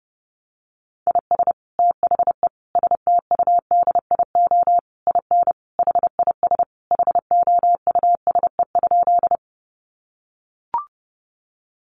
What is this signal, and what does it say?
Morse code, heard as a keyed tone, about 30 wpm, 700 Hz. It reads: IS THE STUDIO IN HIS HOUSE?
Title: Sentences from Top 1200 Words 30wpm